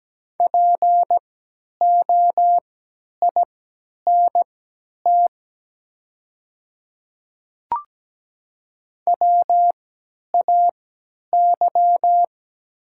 Morse code, heard as a keyed tone, about 17 wpm, spaced as for 10 wpm, 700 Hz. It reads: POINT WAY
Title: 3x Character Spacing - Top 300 Words - 17wpm